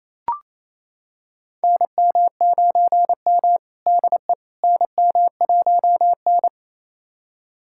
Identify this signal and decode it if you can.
Morse code, heard as a keyed tone, about 28 wpm, 700 Hz. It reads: NM9M DE NM1N
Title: USA Callsign DE Callsign - 4 Characters - 28wpm